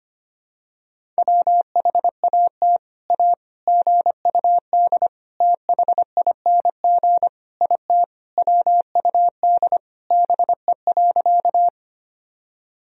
Morse code, heard as a keyed tone, about 25 wpm, 700 Hz. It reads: WHAT A GUD THING IT WUD BE.
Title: Sentences from Top 200 Words and Abbreviations 25wpm